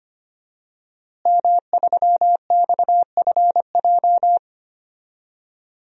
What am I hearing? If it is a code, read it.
Morse code, heard as a keyed tone, about 25 wpm, 700 Hz. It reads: M3XFJ
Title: DXCC-100 Call Signs - 5 Characters - Repeated 2x - 25wpm